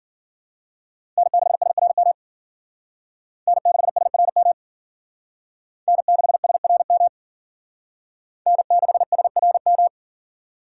Morse code, heard as a keyed tone, about 51 wpm, 700 Hz. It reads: N6SRM N6SRM N6SRM N6SRM